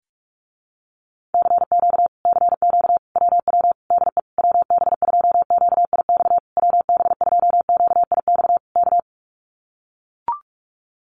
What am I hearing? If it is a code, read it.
Morse code, heard as a keyed tone, about 44 wpm, 700 Hz. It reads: CQ CQ WW DE WB2QIX WB2QIX K
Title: Common QSO Phrases Encore 45wpm